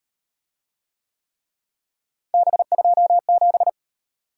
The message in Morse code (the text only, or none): B27